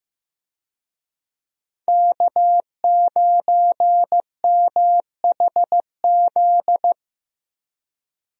Morse code, heard as a keyed tone, about 15 wpm, 700 Hz. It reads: K9MHZ
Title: USA Call Signs - 5 Characters - Repeated 2x - 15wpm